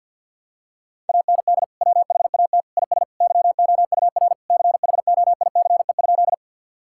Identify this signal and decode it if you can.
Morse code, heard as a keyed tone, about 47 wpm, 700 Hz. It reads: AND WHAT IS YOUR CHOICE?
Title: Sentences from Top 1000 Words - Review Part 6 - 50wpm